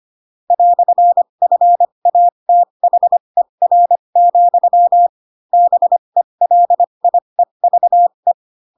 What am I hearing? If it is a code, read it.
Morse code, heard as a keyed tone, about 25 wpm, spaced as for 22 wpm, 700 Hz. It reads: "FATHER, BELIEVE
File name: Pinocchio / Pinocchio ch7 18 25 ik1hge